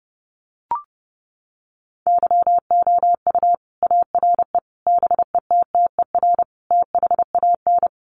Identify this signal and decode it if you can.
Morse code, heard as a keyed tone, about 28 wpm, 700 Hz. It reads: YOU ARE BETTER THAN
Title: Sentences from Top 400 Words 30wpm